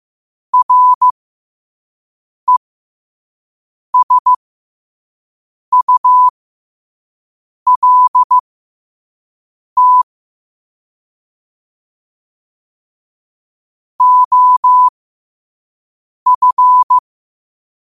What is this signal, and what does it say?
Morse code, heard as a keyed tone, about 15 wpm, spaced as for 5 wpm, 1000 Hz. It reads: RESULT OF
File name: qotdfiles / 5wpm